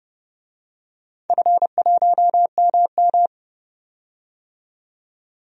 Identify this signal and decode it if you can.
Morse code, heard as a keyed tone, about 29 wpm, 700 Hz. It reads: F1MM
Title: DXCC-100 Call Signs - 4 Characters - 30wpm